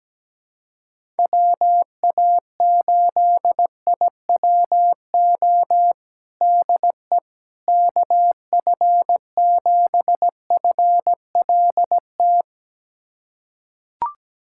Morse code, heard as a keyed tone, about 17 wpm, 700 Hz. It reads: WA8IWO DE KF7FLT
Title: USA Callsign DE Callsign 17wpm